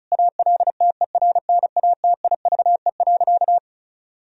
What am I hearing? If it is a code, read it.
Morse code, heard as a keyed tone, about 36 wpm, 700 Hz. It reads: ALTERNATIVE.
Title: Sentences from Top 1000 Words - Review Part 4 - 35wpm